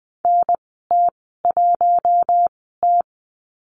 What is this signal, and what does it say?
Morse code, heard as a keyed tone, about 20 wpm, spaced as for 14 wpm, 700 Hz. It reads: NT1T
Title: 048 Lesson 9g - Number 4 - Call Signs 20wpm